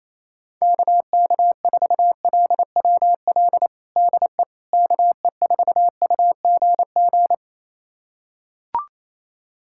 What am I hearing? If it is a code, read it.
Morse code, heard as a keyed tone, about 28 wpm, 700 Hz. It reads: KK4LWL DE KE4UGG
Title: USA Callsign DE Callsign - 6 Characters - 28wpm